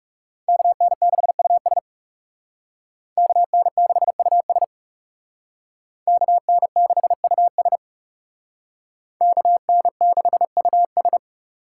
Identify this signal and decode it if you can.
Morse code, heard as a keyed tone, about 36 wpm, 700 Hz. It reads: KN6US KN6US KN6US KN6US
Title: Speed Racing - USA 2x2 Call Signs 30wpm